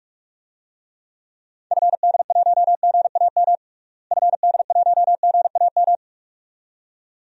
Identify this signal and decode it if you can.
Morse code, heard as a keyed tone, about 41 wpm, 700 Hz. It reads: FD1GAM FD1GAM
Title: DXCC-100 Call Signs - 6 Characters - Repeated 2x - 45wpm